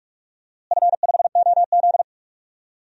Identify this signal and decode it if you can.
Morse code, heard as a keyed tone, about 46 wpm, 700 Hz. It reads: F5OZ
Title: DXCC-100 Call Signs - 4 Characters - Repeated 2x - 45wpm